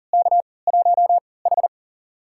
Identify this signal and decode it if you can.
Morse code, heard as a keyed tone, about 40 wpm, spaced as for 23 wpm, 700 Hz. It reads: K1H
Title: USA Call Signs - 3x Character Spacing - 40wpm